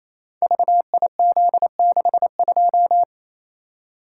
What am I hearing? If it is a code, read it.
Morse code, heard as a keyed tone, about 28 wpm, 700 Hz. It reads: VIZ62
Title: Code Groups - 5 Letter:Numbers - 28wpm